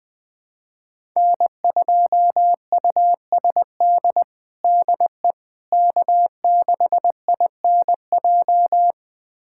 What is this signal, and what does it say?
Morse code, heard as a keyed tone, about 20 wpm, 700 Hz. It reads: N2USD DE K6INJ
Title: USA Callsign DE Callsign - 5 Characters - 20wpm